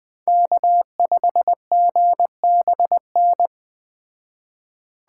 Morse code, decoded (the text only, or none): K5GBN